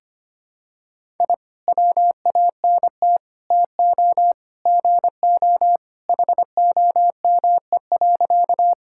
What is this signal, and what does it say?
Morse code, heard as a keyed tone, about 25 wpm, 700 Hz. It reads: I WANT TO GO HOME.
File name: practice / 25wpm / Sentences with 3-Word Phrases - 25wpm